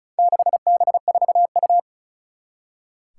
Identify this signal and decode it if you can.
Morse code, heard as a keyed tone, about 34 wpm, 700 Hz. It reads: 6B4U
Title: Four Challenging Letter-Number Combinations 35wpm